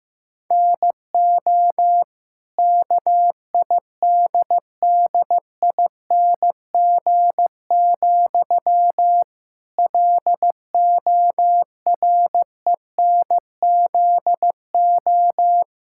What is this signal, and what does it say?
Morse code, heard as a keyed tone, about 14 wpm, 700 Hz. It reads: NO KIDDING, LORENZO
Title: Pangram Challenge v2 - 15wpm